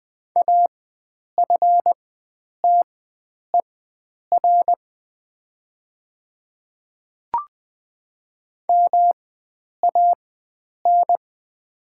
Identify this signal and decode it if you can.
Morse code, heard as a keyed tone, about 20 wpm, spaced as for 9 wpm, 700 Hz. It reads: AFTER MAN
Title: 4x Character Spacing - Top 300 Words - 20wpm